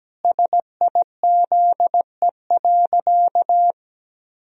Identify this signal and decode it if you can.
Morse code, heard as a keyed tone, about 17 wpm, 700 Hz. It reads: SIZE.
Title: Sentences from Top 1000 Words - Review Part 3 - 17wpm